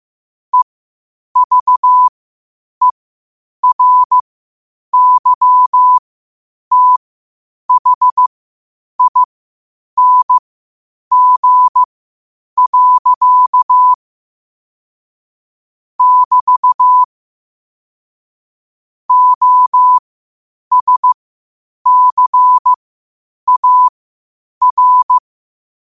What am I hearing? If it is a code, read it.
Morse code, heard as a keyed tone, about 15 wpm, spaced as for 8 wpm, 1000 Hz. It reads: EVERYTHING. <BT> OSCAR
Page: qotd-at-8-wpm